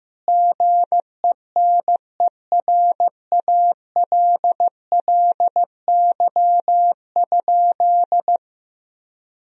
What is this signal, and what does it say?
Morse code, heard as a keyed tone, about 15 wpm, 700 Hz. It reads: GENERALLY?
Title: Sentences from Top 1000 Words - Review Part 4 - 15wpm